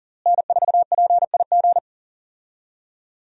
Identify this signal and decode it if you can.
Morse code, heard as a keyed tone, about 40 wpm, 700 Hz. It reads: N4PIG
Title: USA Call Signs - 5 Characters - Repeated 2x - 40wpm